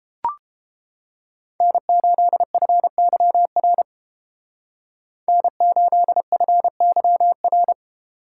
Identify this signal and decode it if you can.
Morse code, heard as a keyed tone, about 31 wpm, 700 Hz. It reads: N8FYR N8FYR